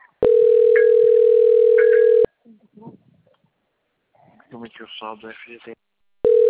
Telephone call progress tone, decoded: ringback tone

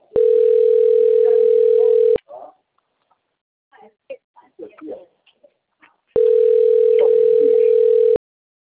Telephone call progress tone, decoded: ringback tone

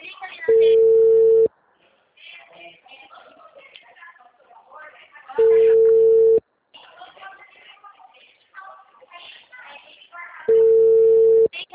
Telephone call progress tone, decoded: ringback tone

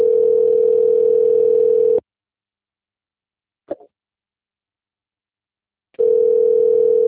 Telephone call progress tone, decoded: ringback tone

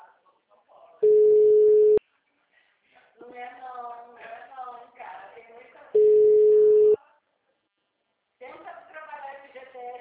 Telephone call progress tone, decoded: ringback tone

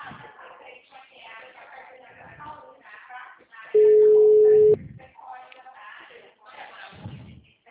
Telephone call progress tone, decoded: ringback tone

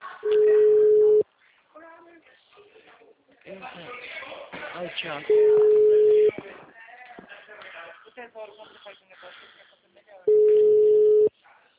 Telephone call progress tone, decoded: ringback tone

